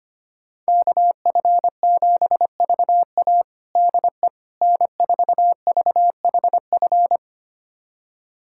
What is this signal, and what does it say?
Morse code, heard as a keyed tone, about 25 wpm, 700 Hz. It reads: KF7VA DE N4VHF
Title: USA Callsign DE Callsign - 5 Characters - 25wpm